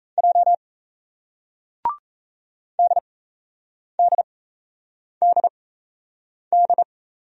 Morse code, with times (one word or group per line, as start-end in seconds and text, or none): 0.2-0.5 J
2.8-6.8 DDDD